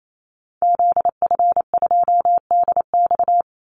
Morse code, heard as a keyed tone, about 28 wpm, 700 Hz. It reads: ZF2DX
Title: Call Signs - 28wpm